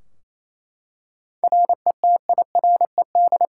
Morse code, heard as a keyed tone, about 28 wpm, 700 Hz. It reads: RETIRED
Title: K1USN SST QSO Practice 28wpm